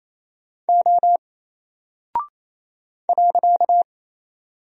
Morse code, heard as a keyed tone, about 29 wpm, 700 Hz. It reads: O .
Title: Single Character (Letter:Number:Punctuation:Prosign) Alt 28wpm